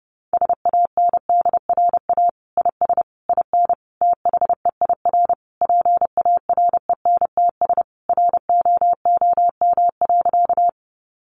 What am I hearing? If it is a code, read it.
Morse code, heard as a keyed tone, about 30 wpm, 700 Hz. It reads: SANDRA IS IN THEIR PARENTS ROOM.
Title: Easy Phrases 30wpm